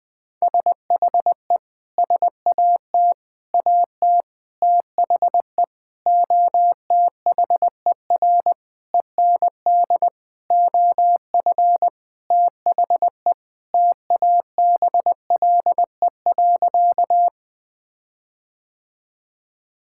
Morse code, heard as a keyed tone, about 21 wpm, 700 Hz. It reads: SHE SAT AT THE OTHER END OF THE TABLE.
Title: Sentences with 6-Word Phrases - 20wpm